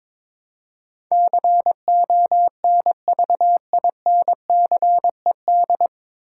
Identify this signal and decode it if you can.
Morse code, heard as a keyed tone, about 22 wpm, 700 Hz. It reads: CONVINCED
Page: Morse Code Ninja practice 22wpm